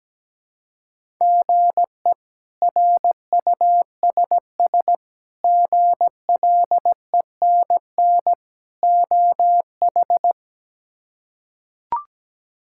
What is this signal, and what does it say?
Morse code, heard as a keyed tone, about 17 wpm, 700 Hz. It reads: GE RUSS GLENN OH
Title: K1USN SST QSO Practice - 17wpm